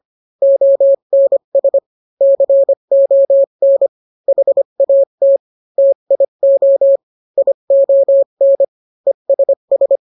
Morse code, heard as a keyed tone, about 25 wpm, 550 Hz. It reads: ONS CON HAT TIO ION ESS